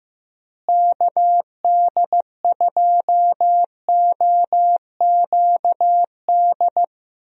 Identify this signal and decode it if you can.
Morse code, heard as a keyed tone, about 15 wpm, 700 Hz. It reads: KD2OQD
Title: USA Call Signs - 6 Characters - Repeated 3x - 15wpm